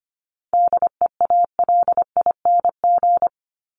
Morse code, heard as a keyed tone, about 25 wpm, 700 Hz. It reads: DEALING
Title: ING Suffix - Top 100 Words with Base Words 25wpm